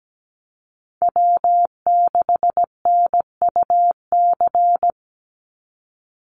Morse code, heard as a keyed tone, about 17 wpm, 700 Hz. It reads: W6NUC
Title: Call Signs Repeated 2x - 17wpm